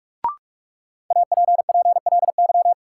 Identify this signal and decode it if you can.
Morse code, heard as a keyed tone, about 46 wpm, 700 Hz. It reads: APPLY
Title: Sentences from Top 1200 Words - New Word Drill Alt - 45wpm